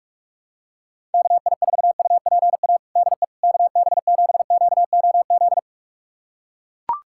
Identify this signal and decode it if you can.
Morse code, heard as a keyed tone, about 42 wpm, 700 Hz. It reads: KI4UPA DE KB7QOZ
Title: USA Callsign DE Callsign 45wpm